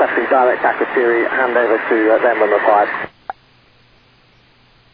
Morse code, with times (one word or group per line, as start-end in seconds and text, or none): none